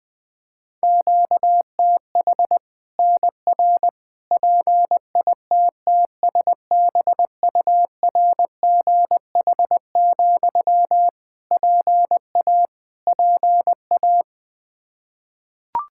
Morse code, heard as a keyed tone, about 21 wpm, 700 Hz. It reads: QTH NR PITTSBURGH, PA PA
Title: Practice QSO - Exchange 1 - Beginning - 20wpm